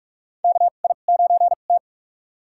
Morse code, heard as a keyed tone, about 46 wpm, spaced as for 33 wpm, 700 Hz. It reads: KI9T
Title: USA Call Signs - 2x Character Spacing - 45wpm